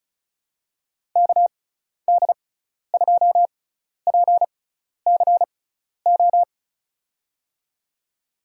Morse code, heard as a keyed tone, about 34 wpm, spaced as for 12 wpm, 690 Hz. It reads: KD2PCO
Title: USA Call Signs - 6 Characters - 6x Character Spacing - 35wpm